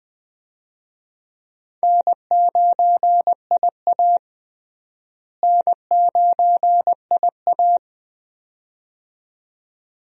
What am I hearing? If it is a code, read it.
Morse code, heard as a keyed tone, about 20 wpm, 700 Hz. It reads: N9IA N9IA
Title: USA Call Signs - 4 Characters - Repeated 2x - 20wpm